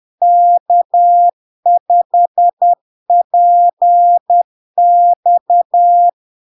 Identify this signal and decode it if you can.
Morse code, heard as a keyed tone, about 10 wpm, 700 Hz. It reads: K5PX
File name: LARK-FD-2022-Log-in-CW-10wpm-700-Hz